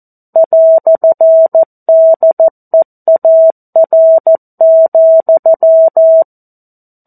Morse code, heard as a keyed tone, about 14 wpm, 650 Hz. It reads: "DEAR,
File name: 02-Chapter-02 14 WPM